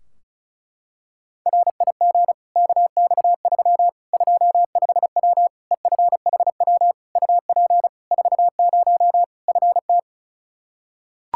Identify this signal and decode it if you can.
Morse code, heard as a keyed tone, about 36 wpm, 700 Hz. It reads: RIG KX3 25W EFHW UP 40 FT
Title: Practice QSO - Exchange 2 - Info - 35wpm